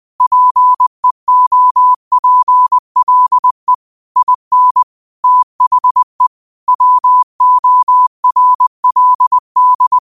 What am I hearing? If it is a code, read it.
Morse code, heard as a keyed tone, about 20 wpm, 1000 Hz. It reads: PEOPLE IN THE WORLD
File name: qotdfiles / 20wpm